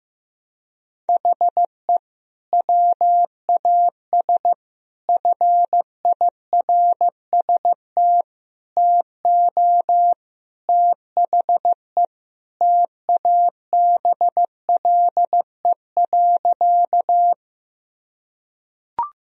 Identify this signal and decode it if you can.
Morse code, heard as a keyed tone, about 15 wpm, 700 Hz. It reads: HE WAS FIRST TO THE TABLE.
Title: Sentences from Top 1000 Words - Review Part 2 - 15wpm